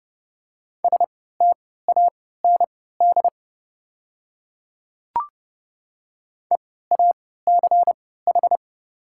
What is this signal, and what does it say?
Morse code, heard as a keyed tone, about 30 wpm, spaced as for 17 wpm, 700 Hz. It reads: STAND EACH
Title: 3x Character Spacing - Top 300 Words - 30wpm